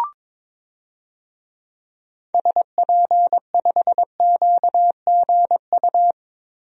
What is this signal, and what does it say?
Morse code, heard as a keyed tone, about 22 wpm, 700 Hz. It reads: SP5QGU
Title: DXCC-100 Call Signs - 6 Characters - Repeated 3x - 22wpm